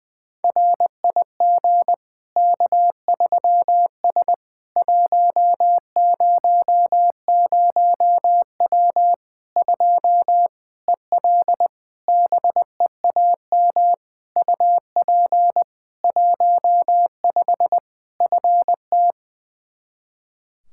Morse code, heard as a keyed tone, about 20 wpm, 700 Hz. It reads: RIG K3S 100W 2 EL BEAM UP 15 FT
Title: Practice QSO - Exchange 2 - Info - 20wpm